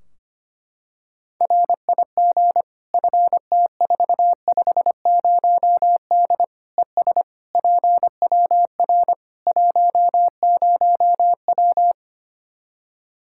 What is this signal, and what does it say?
Morse code, heard as a keyed tone, about 25 wpm, 700 Hz. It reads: RIG FT450D ES PWR 10W
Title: Practice QSO - Exchange 2 - Info - 25wpm